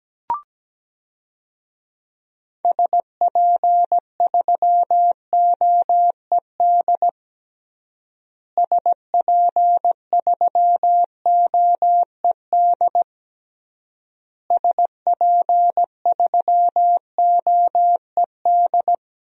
Morse code, heard as a keyed tone, about 17 wpm, 700 Hz. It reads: SP3OED SP3OED SP3OED
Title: DXCC-100 Call Signs - 6 Characters - Repeated 3x - 17wpm